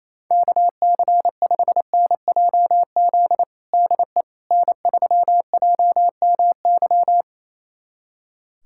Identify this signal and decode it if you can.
Morse code, heard as a keyed tone, about 28 wpm, 700 Hz. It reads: KC5NJZ DE N3JMY
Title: USA Callsign DE Callsign 28wpm